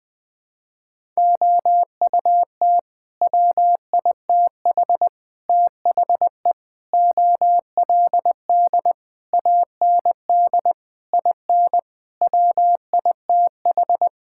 Morse code, heard as a keyed tone, about 21 wpm, 700 Hz. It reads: OUT WITH THE OLD AND IN WITH
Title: Sets of 9 Words - 20wpm